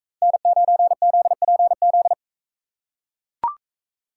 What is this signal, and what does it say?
Morse code, heard as a keyed tone, about 43 wpm, 700 Hz. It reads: N9ZPZ